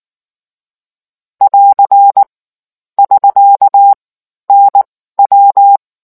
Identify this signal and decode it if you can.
Morse code, heard as a keyed tone, about 19 wpm, spaced as for 14 wpm, 800 Hz. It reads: <AR> <SK> NW